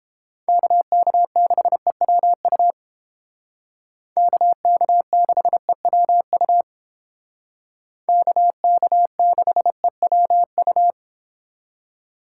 Morse code, heard as a keyed tone, about 30 wpm, 700 Hz. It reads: KK6EWU KK6EWU KK6EWU